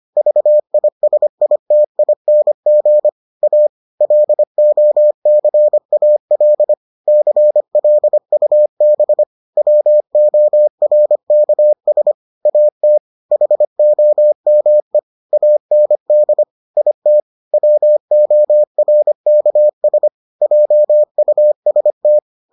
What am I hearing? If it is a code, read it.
Morse code, heard as a keyed tone, about 25 wpm, 600 Hz. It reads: VISITING A LOCAL CLUB WORKS AT HOME AND IT WORKS JUST